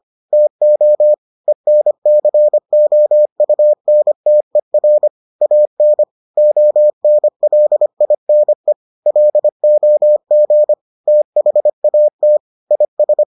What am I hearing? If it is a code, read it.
Morse code, heard as a keyed tone, about 25 wpm, 600 Hz. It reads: TO ENCOUNTER AN ONLINE LOG THAT IS